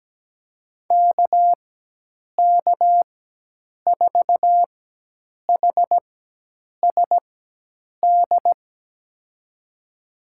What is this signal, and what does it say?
Morse code, heard as a keyed tone, about 17 wpm, spaced as for 8 wpm, 700 Hz. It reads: KK4HSD
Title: USA Call Signs - 4x Character Spacing - 17wpm